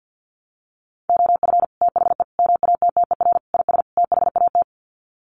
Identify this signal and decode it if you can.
Morse code, heard as a keyed tone, about 50 wpm, 700 Hz. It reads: OF THE MATTER IS THAT
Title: Sets of 5 Words Encore 50wpm